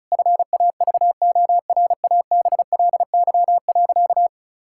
Morse code, heard as a keyed tone, about 36 wpm, 700 Hz. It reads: FAVORABLY.
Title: Pangram Challenge v2 - 35wpm